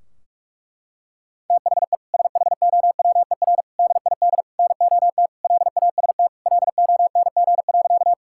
Morse code, heard as a keyed tone, about 46 wpm, 700 Hz. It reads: THE SHOWER DID NOT LAST LONG.
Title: Sentences from Top 1600 Words 45wpm